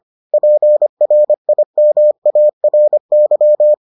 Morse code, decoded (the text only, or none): PRIMARY